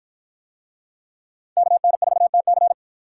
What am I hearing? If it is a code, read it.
Morse code, heard as a keyed tone, about 47 wpm, 700 Hz. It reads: KN4TC